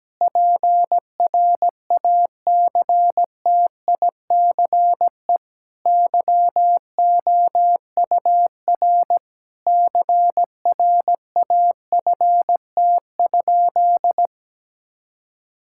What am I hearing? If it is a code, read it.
Morse code, heard as a keyed tone, about 17 wpm, 700 Hz. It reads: PRACTICE YOUR CRAFT?
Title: Sentences from Top 1000 Words - Review Part 6 - 17wpm